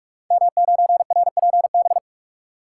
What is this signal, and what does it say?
Morse code, heard as a keyed tone, about 44 wpm, 690 Hz. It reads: M9RPB
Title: Code Groups - 5 Letter:Numbers - 45wpm